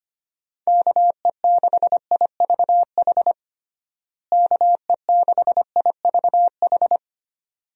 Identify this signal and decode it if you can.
Morse code, heard as a keyed tone, about 25 wpm, 700 Hz. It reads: KE6IVH KE6IVH
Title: USA Call Signs - 6 Characters - Repeated 2x - 25wpm